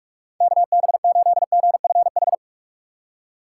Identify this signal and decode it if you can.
Morse code, heard as a keyed tone, about 60 wpm, spaced as for 53 wpm, 700 Hz. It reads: KB8GFH
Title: Speed Racing - USA 2x3 Call Signs 45wpm